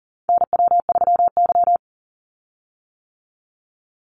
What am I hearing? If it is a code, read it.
Morse code, heard as a keyed tone, about 40 wpm, 700 Hz. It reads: NW3Y
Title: Call Signs - 40wpm